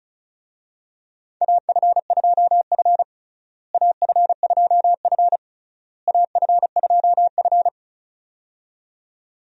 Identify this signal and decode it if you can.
Morse code, heard as a keyed tone, about 36 wpm, 700 Hz. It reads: AF2F AF2F AF2F
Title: USA Call Signs - 4 Characters - Repeated 3x - 35wpm